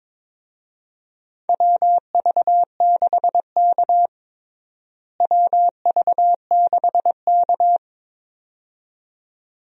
Morse code, heard as a keyed tone, about 22 wpm, 700 Hz. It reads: WV6K WV6K
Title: USA Call Signs - 4 Characters - Repeated 2x - 22wpm